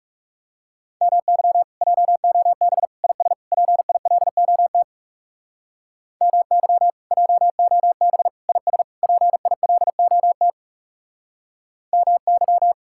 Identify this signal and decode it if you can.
Morse code, heard as a keyed tone, about 39 wpm, 700 Hz. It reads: MY JOB IS PILOT MY JOB IS PILOT MY